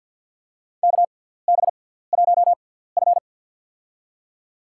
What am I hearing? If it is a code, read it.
Morse code, heard as a keyed tone, about 47 wpm, spaced as for 17 wpm, 700 Hz. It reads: KB1F
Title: USA Call Signs - 4 Characters - 6x Character Spacing - 50wpm